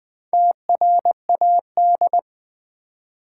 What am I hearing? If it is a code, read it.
Morse code, heard as a keyed tone, about 20 wpm, 700 Hz. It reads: TRAD
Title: Build A Word - 1 Letter At A Time - 3 to 5-Letter Words - 20wpm